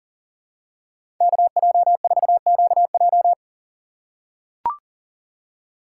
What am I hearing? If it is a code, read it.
Morse code, heard as a keyed tone, about 40 wpm, 700 Hz. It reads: KJ4QJ
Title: USA Call Signs - 5 Characters - 2x Character Spacing - 40wpm